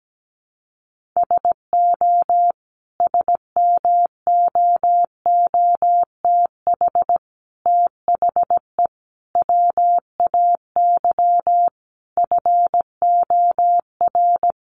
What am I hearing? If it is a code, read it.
Morse code, heard as a keyed tone, about 17 wpm, 700 Hz. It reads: SO SMOOTH THE WAY FOR
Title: Sentences from Top 1400 Words 17wpm